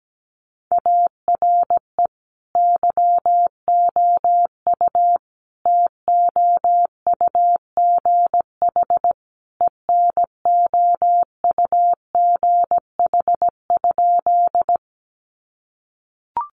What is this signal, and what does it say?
Morse code, heard as a keyed tone, about 17 wpm, 700 Hz. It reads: ARE YOU TOUGH ENOUGH?
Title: Sentences from Top 1200 Words 17wpm